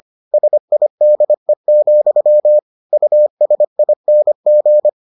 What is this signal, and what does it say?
Morse code, heard as a keyed tone, about 25 wpm, 600 Hz. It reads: SIDE, USING